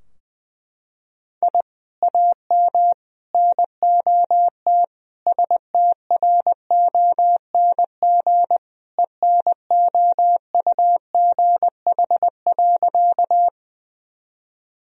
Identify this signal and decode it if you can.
Morse code, heard as a keyed tone, about 20 wpm, 700 Hz. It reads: I AM NOT STRONG ENOUGH.
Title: Sentences from Top 1000 Words - Review Part 3 - 20wpm